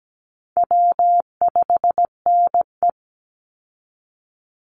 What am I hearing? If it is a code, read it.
Morse code, heard as a keyed tone, about 17 wpm, 700 Hz. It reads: W5NE
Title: Easy Call Signs Repeated 2x - 17wpm